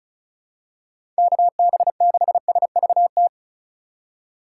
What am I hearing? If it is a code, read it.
Morse code, heard as a keyed tone, about 36 wpm, 700 Hz. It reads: KB6SVT
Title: USA Call Signs - 6 Characters - 3x Character Spacing - 35wpm